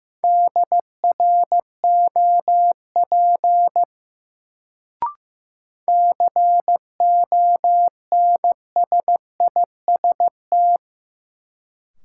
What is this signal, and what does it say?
Morse code, heard as a keyed tone, about 15 wpm, 700 Hz. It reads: DROP CONSIST
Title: Sentences from Top 1400 Words - New Word Drill - 15wpm